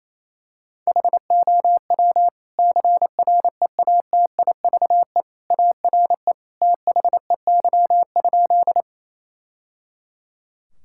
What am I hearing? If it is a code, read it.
Morse code, heard as a keyed tone, about 28 wpm, 700 Hz. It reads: HOW CREATIVE ARE THEY?
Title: Sentences from Top 1000 Words - Review Part 5 - 28wpm